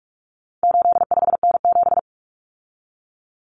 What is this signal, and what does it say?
Morse code, heard as a keyed tone, about 44 wpm, 700 Hz. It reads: 85N7
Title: Common Cut Numbers - 4-Digit 45wpm